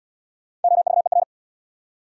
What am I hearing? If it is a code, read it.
Morse code, heard as a keyed tone, about 77 wpm, 700 Hz. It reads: OVER